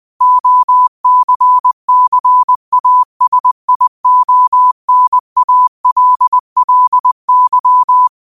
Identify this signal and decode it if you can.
Morse code, heard as a keyed tone, about 20 wpm, 1000 Hz. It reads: OCCASIONALLY